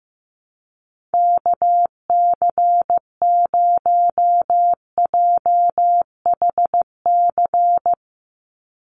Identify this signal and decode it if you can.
Morse code, heard as a keyed tone, about 15 wpm, 700 Hz. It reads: KC0JHC
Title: USA Call Signs 15wpm